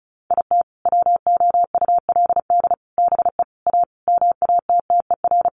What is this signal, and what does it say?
Morse code, heard as a keyed tone, about 34 wpm, 700 Hz. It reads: IT WOULD BE A MATTER
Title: Sentences from Top 1400 Words 35wpm